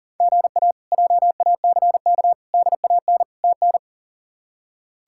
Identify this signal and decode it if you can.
Morse code, heard as a keyed tone, about 40 wpm, 700 Hz. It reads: GA JACK DAN TN
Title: K1USN SST QSO Practice - 40wpm